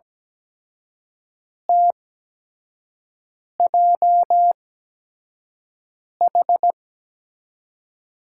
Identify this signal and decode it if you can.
Morse code, heard as a keyed tone, about 17 wpm, spaced as for 5 wpm, 700 Hz. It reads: TJH
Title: USA Call Signs - 5 Characters - 8x Character Spacing - 17wpm